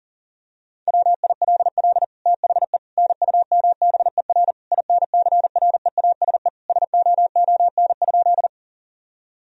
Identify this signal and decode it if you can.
Morse code, heard as a keyed tone, about 40 wpm, 700 Hz. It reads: WILL THE NUMBER INCREASE SOON?
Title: Sentences from Top 1000 Words - Review Part 4 - 40wpm